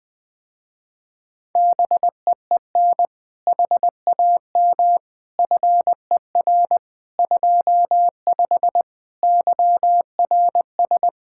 Morse code, heard as a keyed tone, about 20 wpm, 700 Hz. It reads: BEEN HAM FER 25 YRS
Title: Speed-Racing Common QSO Phrases - 17wpm